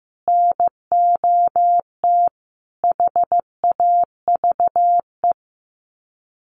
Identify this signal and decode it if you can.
Morse code, heard as a keyed tone, about 15 wpm, 700 Hz. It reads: NOT HAVE
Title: Sets of 4 Words 15wpm